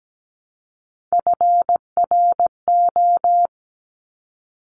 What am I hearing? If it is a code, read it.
Morse code, heard as a keyed tone, about 17 wpm, 700 Hz. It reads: FRO
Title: Top Three Letter Combinations in Words 17wpm